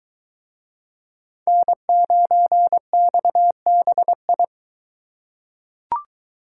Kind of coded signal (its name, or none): Morse code